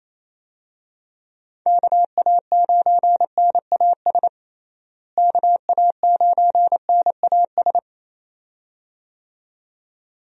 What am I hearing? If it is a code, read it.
Morse code, heard as a keyed tone, about 28 wpm, 700 Hz. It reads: KA9NAS KA9NAS
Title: USA Call Signs - 6 Characters - Repeated 2x - 28wpm